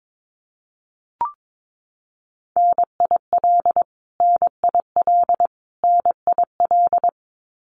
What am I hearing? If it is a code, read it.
Morse code, heard as a keyed tone, about 22 wpm, 700 Hz. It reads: NIL NIL NIL
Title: Abbreviations Repeated 3x - 22wpm